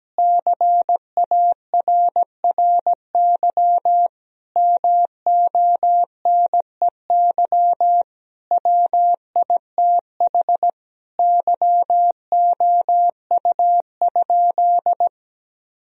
Morse code, heard as a keyed tone, about 17 wpm, 700 Hz. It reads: CARRY MONEY WITH YOU?
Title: Sentences from Top 1000 Words - Review Part 1 - 17wpm